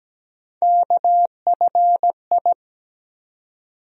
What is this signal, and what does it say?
Morse code, heard as a keyed tone, about 17 wpm, 700 Hz. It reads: KFI
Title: USA Call Sign Postfixes - 3 Letters 17wpm